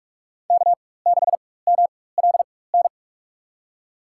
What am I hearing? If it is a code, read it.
Morse code, heard as a keyed tone, about 46 wpm, spaced as for 21 wpm, 700 Hz. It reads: K6MLN
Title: USA Call Signs - 5 Characters - 4x Character Spacing - 45wpm